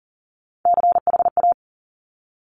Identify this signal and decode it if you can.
Morse code, heard as a keyed tone, about 39 wpm, 700 Hz. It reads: CHA
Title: Top Three Letter Combinations in Words 40wpm